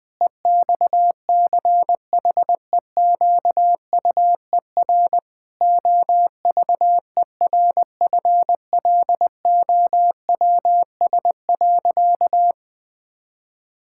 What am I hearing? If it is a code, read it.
Morse code, heard as a keyed tone, about 20 wpm, 700 Hz. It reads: EXCHEQUER OVERFLOWS.
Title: Pangram Challenge v2 - 20wpm